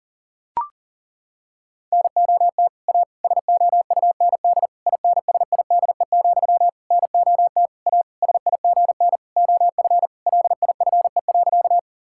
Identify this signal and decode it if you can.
Morse code, heard as a keyed tone, about 40 wpm, 700 Hz. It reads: NOT A SOUND INSIDE, NOT A SIGN OF LIFE.
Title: Sentences from Top 1000 Words - Review Part 6 - 40wpm